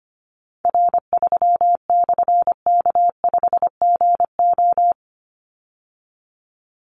Morse code, heard as a keyed tone, about 25 wpm, 700 Hz. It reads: R3/K5GO
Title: Call Signs Repeated 3x - 25wpm